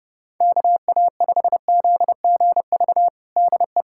Morse code, heard as a keyed tone, about 30 wpm, 700 Hz. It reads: KA5ZGV DE
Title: USA Callsign DE Callsign - 6 Characters - 30wpm